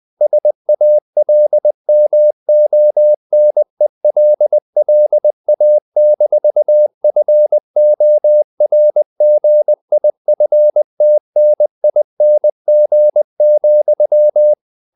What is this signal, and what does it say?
Morse code, heard as a keyed tone, about 20 wpm, 600 Hz. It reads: SALMONELLA-FORGIFTNING,
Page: Ferie 20 wpm